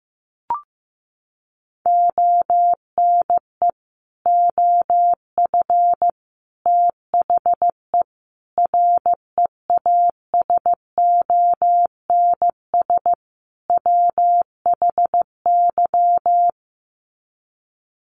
Morse code, heard as a keyed tone, about 15 wpm, 700 Hz. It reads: ONE OF THE REASONS WHY
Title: Sets of 5 Words 15wpm